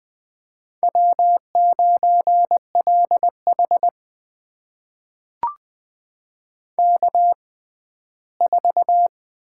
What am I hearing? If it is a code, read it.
Morse code, heard as a keyed tone, about 20 wpm, spaced as for 14 wpm, 700 Hz. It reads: W9LH K4
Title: USA Call Signs - 4 Characters - 6x Character Spacing - 20wpm